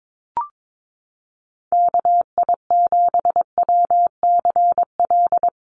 Morse code, heard as a keyed tone, about 22 wpm, 700 Hz. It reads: KI7WCL
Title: USA Call Signs Repeated 3x 22wpm